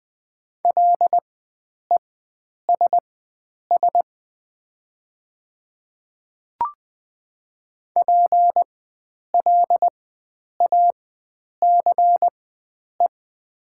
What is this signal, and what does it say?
Morse code, heard as a keyed tone, about 20 wpm, spaced as for 9 wpm, 700 Hz. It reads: LESS PLACE
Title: 4x Character Spacing - Top 500 Words - 20wpm